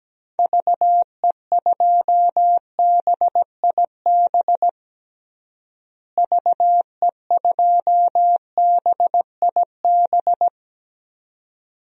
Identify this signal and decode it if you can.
Morse code, heard as a keyed tone, about 17 wpm, 700 Hz. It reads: VE2BIB VE2BIB
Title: DXCC-100 Call Signs - 6 Characters - Repeated 2x - 17wpm